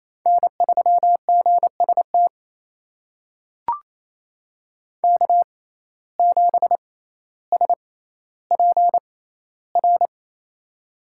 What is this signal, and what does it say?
Morse code, heard as a keyed tone, about 28 wpm, spaced as for 14 wpm, 700 Hz. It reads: N3GST K7SPR
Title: USA Call Signs - 5 Characters - 6x Character Spacing - 28wpm